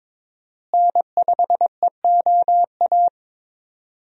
Morse code, heard as a keyed tone, about 22 wpm, 700 Hz. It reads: N5EOA
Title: USA Call Signs - 5 Characters - Repeated 2x - 22wpm